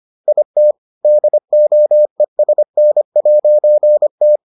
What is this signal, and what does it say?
Morse code, heard as a keyed tone, about 25 wpm, 600 Hz. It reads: IT DOESN'T